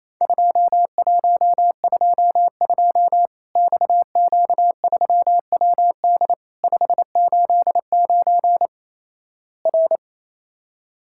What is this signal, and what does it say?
Morse code, heard as a keyed tone, about 28 wpm, 700 Hz. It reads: 2122 XQ3WD 589 R
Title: Worked All Europe DX Contest - QTC Exchange - 28wpm